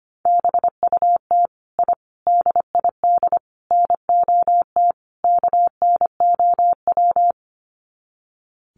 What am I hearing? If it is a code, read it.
Morse code, heard as a keyed tone, about 25 wpm, 700 Hz. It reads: BUT I DID NOT KNOW
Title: Sets of 5 Words Encore 25wpm